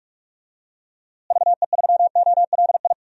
Morse code, heard as a keyed tone, about 46 wpm, 700 Hz. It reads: VE3OLI
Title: DXCC-100 Call Signs - 6 Characters - Repeated 2x - 45wpm